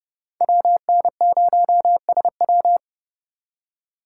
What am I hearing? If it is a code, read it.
Morse code, heard as a keyed tone, about 30 wpm, 700 Hz. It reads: WN0SW
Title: USA Call Signs - 5 Characters - 8x Character Spacing - 30wpm